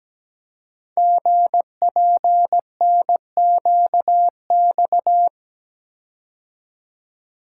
Morse code, heard as a keyed tone, about 17 wpm, 700 Hz. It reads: GPNQX
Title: Code Groups - 5 Letter:Numbers - 17wpm